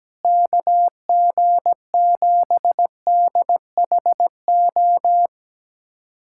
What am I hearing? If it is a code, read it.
Morse code, heard as a keyed tone, about 17 wpm, 700 Hz. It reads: KG7DHO
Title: USA Callsign DE Callsign - 6 Characters - 17wpm